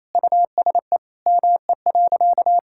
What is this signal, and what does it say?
Morse code, heard as a keyed tone, about 27 wpm, 700 Hz. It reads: USE ME.
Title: Sentences from Top 1000 Words - Review Part 6 - 28wpm